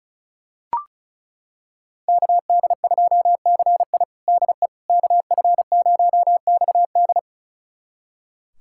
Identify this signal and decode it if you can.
Morse code, heard as a keyed tone, about 34 wpm, 700 Hz. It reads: KD2CI DE KF0XD
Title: USA Callsign DE Callsign - 5 Characters - 35wpm